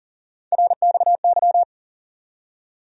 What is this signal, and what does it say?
Morse code, heard as a keyed tone, about 39 wpm, 700 Hz. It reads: RXY
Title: USA Call Sign Postfixes - 3 Letters 40wpm